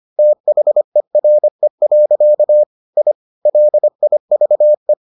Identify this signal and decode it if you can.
Morse code, heard as a keyed tone, about 25 wpm, 600 Hz. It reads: THERE. I LIVE